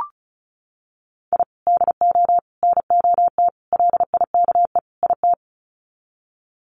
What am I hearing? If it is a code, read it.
Morse code, heard as a keyed tone, about 34 wpm, 700 Hz. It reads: I DO NOT LIKE IT
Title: Sets of 5 Words Encore 35wpm